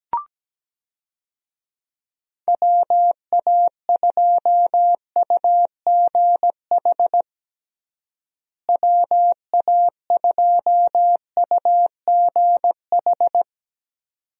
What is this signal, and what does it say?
Morse code, heard as a keyed tone, about 17 wpm, 700 Hz. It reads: WA2UGH WA2UGH
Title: USA Call Signs - 6 Characters - Repeated 2x - 17wpm